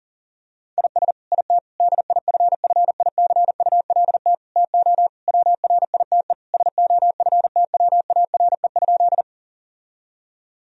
Morse code, heard as a keyed tone, about 40 wpm, 700 Hz. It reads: IS IT DIFFICULT TO WRITE SOFTWARE?
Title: Sentences from Top 1000 Words - Review Part 3 - 40wpm